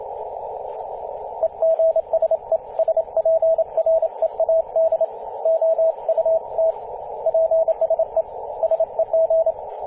Morse code, heard as a keyed tone, about 30 wpm, 650 Hz, with radio noise. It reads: EGSE SPREAD OUT PSE SEG
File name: 5w0xg-18-075mhz-cw